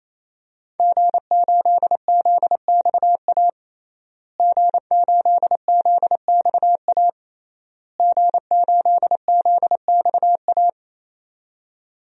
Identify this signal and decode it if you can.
Morse code, heard as a keyed tone, about 28 wpm, 700 Hz. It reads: G8ZXA G8ZXA G8ZXA
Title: DXCC-100 Call Signs - 5 Characters - Repeated 3x - 28wpm